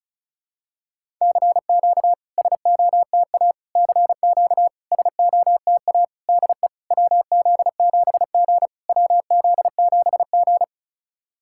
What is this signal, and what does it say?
Morse code, heard as a keyed tone, about 36 wpm, 690 Hz. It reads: CQ SOTA CQ SOTA DE WZ7G WZ7G
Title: SOTA QSO Practice 35wpm